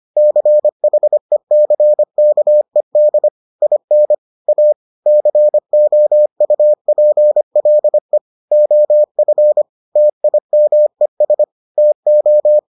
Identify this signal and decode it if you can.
Morse code, heard as a keyed tone, about 25 wpm, 600 Hz. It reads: CHECKED IN A COUPLE OF TIMES TO